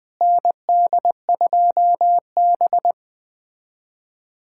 Morse code, heard as a keyed tone, about 20 wpm, 700 Hz. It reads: ND2B
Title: USA Callsign DE Callsign - 4 Characters - 20wpm